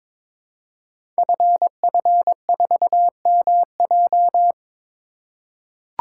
Morse code, heard as a keyed tone, about 22 wpm, 700 Hz. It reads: FF4MJ
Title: Code Groups - 5 Letter:Numbers - 22wpm